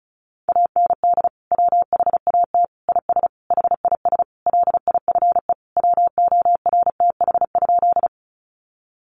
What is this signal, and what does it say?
Morse code, heard as a keyed tone, about 34 wpm, 700 Hz. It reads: AND WHAT IS HIS LIFE WORTH?
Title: Sentences from Top 800 Words 35wpm